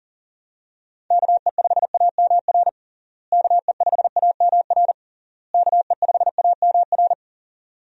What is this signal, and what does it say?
Morse code, heard as a keyed tone, about 40 wpm, 700 Hz. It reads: KE5AMR KE5AMR KE5AMR
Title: USA Call Signs - 6 Characters - Repeated 3x - 40wpm